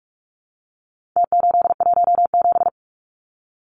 Morse code, heard as a keyed tone, about 44 wpm, 700 Hz. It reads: T817
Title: Numbers - Contest Style - 4-Digit 45wpm